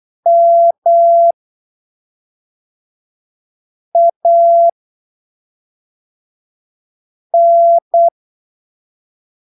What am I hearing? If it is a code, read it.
Morse code, heard as a keyed tone, about 8 wpm, spaced as for 3 wpm, 680 Hz. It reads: MAN
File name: texto 24 08ppm